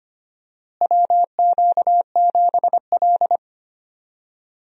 Morse code, heard as a keyed tone, about 25 wpm, 700 Hz. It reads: WQ7L
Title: USA Call Signs - 4 Characters - Repeated 2x - 25wpm